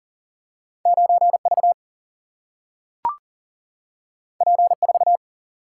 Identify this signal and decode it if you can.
Morse code, heard as a keyed tone, about 40 wpm, 700 Hz. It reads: 9V P4
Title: Single-Letter Single-Number 40wpm